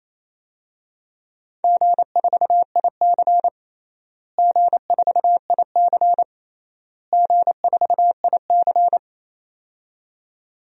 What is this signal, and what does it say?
Morse code, heard as a keyed tone, about 28 wpm, 700 Hz. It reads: G4IC G4IC G4IC
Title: DXCC-100 Call Signs - 4 Characters - Repeated 3x - 28wpm